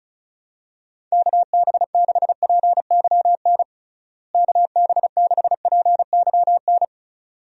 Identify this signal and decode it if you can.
Morse code, heard as a keyed tone, about 34 wpm, 700 Hz. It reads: KB6PYN KB6PYN
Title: USA Call Signs - 6 Characters - Repeated 2x - 35wpm